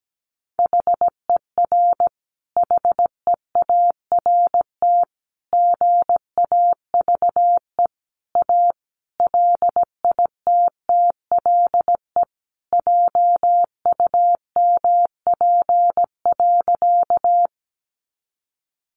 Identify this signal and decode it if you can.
Morse code, heard as a keyed tone, about 17 wpm, 700 Hz. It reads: HER HEART GAVE A LITTLE JUMP.
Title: Sentences from Top 1300 Words 17wpm